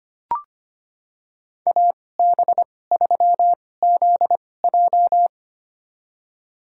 Morse code, heard as a keyed tone, about 25 wpm, spaced as for 18 wpm, 700 Hz. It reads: AB3ZJ
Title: USA Call Signs - 5 Characters - 2x Character Spacing - 25wpm